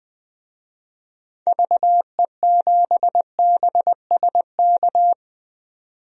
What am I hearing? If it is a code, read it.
Morse code, heard as a keyed tone, about 20 wpm, 700 Hz. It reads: VE7BSK
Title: DXCC-100 Call Signs - 6 Characters - 20wpm